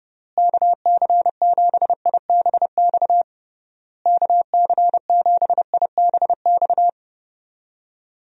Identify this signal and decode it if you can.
Morse code, heard as a keyed tone, about 30 wpm, 700 Hz. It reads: KC7IBX KC7IBX
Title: USA Call Signs - 6 Characters - Repeated 3x - 30wpm